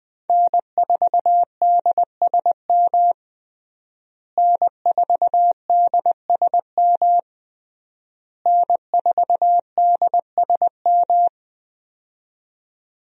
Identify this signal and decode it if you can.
Morse code, heard as a keyed tone, about 20 wpm, 700 Hz. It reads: N4DSM N4DSM N4DSM
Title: USA Call Signs - 5 Characters - Repeated 3x - 20wpm